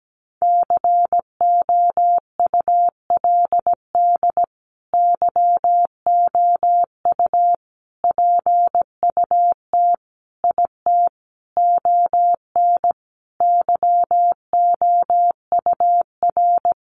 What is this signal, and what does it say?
Morse code, heard as a keyed tone, about 17 wpm, 700 Hz. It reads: COULD YOU PUT IT ON YOUR
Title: Sentences from Top 1400 Words 17wpm